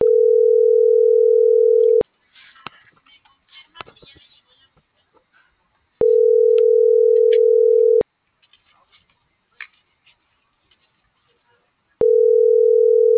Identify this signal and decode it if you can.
Ringback tone (North America), 440 + 480 Hz.